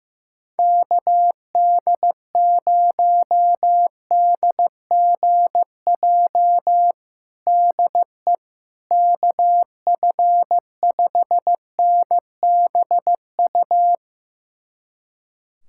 Morse code, heard as a keyed tone, about 15 wpm, 700 Hz. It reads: KD0DGJ DE KF5NBU
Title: USA Callsign DE Callsign - 6 Characters - 15wpm